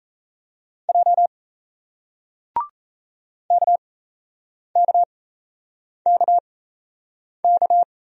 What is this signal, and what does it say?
Morse code, heard as a keyed tone, about 38 wpm, spaced as for 8 wpm, 700 Hz. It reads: J KKKK